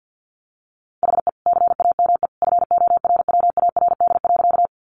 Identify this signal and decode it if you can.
Morse code, heard as a keyed tone, about 50 wpm, 700 Hz. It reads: HE CAME FORWARD.
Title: Sentences from Top 1300 Words 50wpm